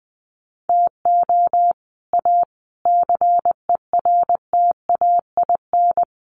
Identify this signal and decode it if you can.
Morse code, heard as a keyed tone, about 20 wpm, 700 Hz. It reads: TO A CERTAIN
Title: Sentences from Top 1200 Words 20wpm